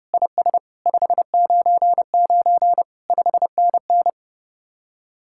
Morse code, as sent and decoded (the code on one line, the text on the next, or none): .. ... / ..... ----. ----. / ..... -. -.
IS 599 5NN